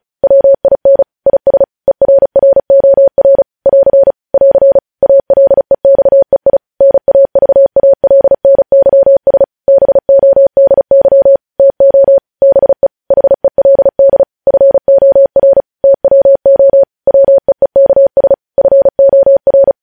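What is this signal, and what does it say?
Morse code, heard as a keyed tone, about 36 wpm, 550 Hz. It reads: WIN IS ERROR <AR> <AR> ALEXEI NAVALNYS BODY TO BE HELD FOR TWO WEEKS FOR
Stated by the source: Morse news headlines practice at 35.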